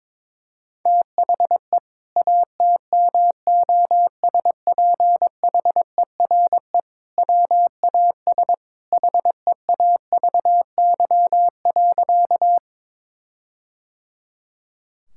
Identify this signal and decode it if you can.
Morse code, heard as a keyed tone, about 22 wpm, 700 Hz. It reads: THE ATMOSPHERE WAS HEAVY.
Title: Sentences from Top 1600 Words 22wpm